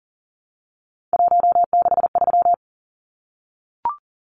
Morse code, heard as a keyed tone, about 40 wpm, 700 Hz. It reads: 163
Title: Numbers - 3-Digit 40wpm